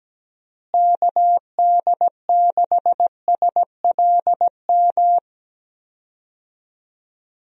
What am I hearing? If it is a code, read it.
Morse code, heard as a keyed tone, about 17 wpm, 700 Hz. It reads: KD6SLM
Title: USA Call Signs - 6 Characters - Repeated 2x - 17wpm